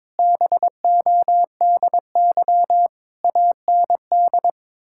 Morse code, heard as a keyed tone, about 22 wpm, 700 Hz. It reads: BODY AND
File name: Difficult Sentences with Hints - 22wpm